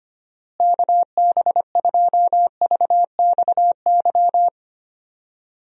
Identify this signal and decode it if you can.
Morse code, heard as a keyed tone, about 25 wpm, 700 Hz. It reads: KB2VXY
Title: USA Call Signs - 6 Characters - Repeated 3x - 25wpm